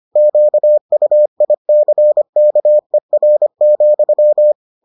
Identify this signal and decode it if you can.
Morse code, heard as a keyed tone, about 25 wpm, 600 Hz. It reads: QUICKER,